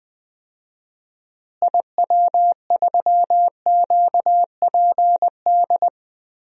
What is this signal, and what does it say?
Morse code, heard as a keyed tone, about 20 wpm, 700 Hz. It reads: IW3QPD
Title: DXCC-100 Call Signs - 6 Characters - Repeated 3x - 20wpm